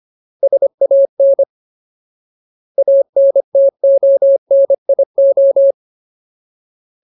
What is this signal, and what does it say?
Morse code, heard as a keyed tone, about 25 wpm, 550 Hz. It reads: SAN ANTONIO
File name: Ultimate Word List / 25WPM Ultimate